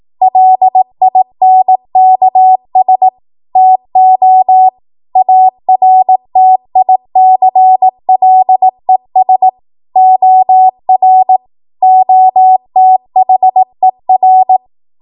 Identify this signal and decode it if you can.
Morse code, heard as a keyed tone, about 18 wpm, 750 Hz. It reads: LINKS TO ARTICLES OR OTHER
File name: Morse / 18WPM